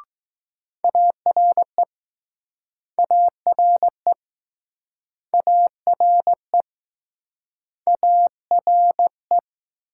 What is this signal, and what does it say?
Morse code, heard as a keyed tone, about 20 wpm, 700 Hz. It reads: ARE ARE ARE ARE